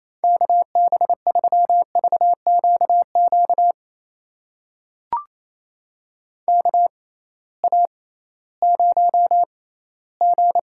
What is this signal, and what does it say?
Morse code, heard as a keyed tone, about 28 wpm, spaced as for 16 wpm, 700 Hz. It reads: KB3VQQ KA0G
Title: USA Call Signs - 6 Characters - 6x Character Spacing - 28wpm